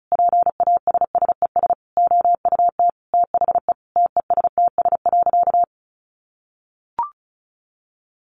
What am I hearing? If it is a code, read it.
Morse code, heard as a keyed tone, about 34 wpm, 700 Hz. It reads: PASSES OUT THE TESTS.
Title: Additional Easy Phrases 35wpm